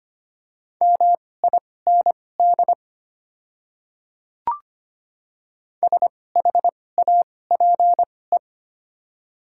Morse code, heard as a keyed tone, about 25 wpm, spaced as for 18 wpm, 700 Hz. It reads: MIND SHAPE
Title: 2x Character Spacing - Top 500 Words - 25wpm